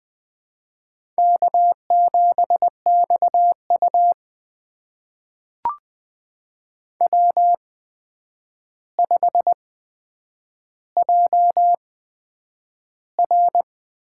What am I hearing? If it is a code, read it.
Morse code, heard as a keyed tone, about 20 wpm, spaced as for 9 wpm, 700 Hz. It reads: K7XU W5JR
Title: USA Call Signs - 4 Characters - 8x Character Spacing - 20wpm